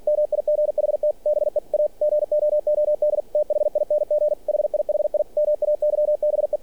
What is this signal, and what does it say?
Morse code, heard as a keyed tone, about 46 wpm, 600 Hz, with radio noise. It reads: MIGHT BE A GOOD THING HIHI MAYBE
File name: archived / CRSnet